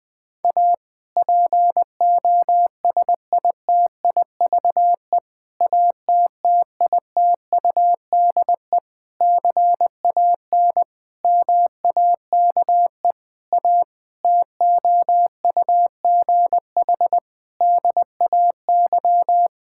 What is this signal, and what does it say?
Morse code, heard as a keyed tone, about 21 wpm, 700 Hz. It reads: A POSITIVE ATTITUDE CAN MAKE A TOUGH DAY
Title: Difficult Sentences with Hints - 20wpm